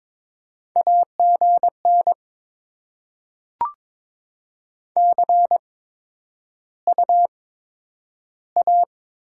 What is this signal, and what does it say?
Morse code, heard as a keyed tone, about 22 wpm, 700 Hz. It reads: A G N C U A
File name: Top 20 QSO Elements - 8x Character Spacing - 22wpm